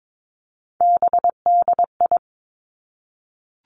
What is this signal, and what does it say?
Morse code, heard as a keyed tone, about 22 wpm, 700 Hz. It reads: BDI
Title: Three Challenging Letter-Number Combinations 22wpm